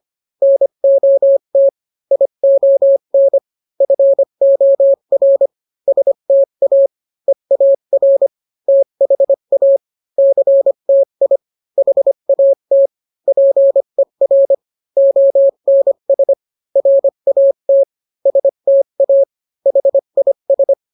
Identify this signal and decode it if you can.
Morse code, heard as a keyed tone, about 25 wpm, 550 Hz. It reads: NOT ION FOR STA EAR THA CTI HAT PER ONS RAT STA HIS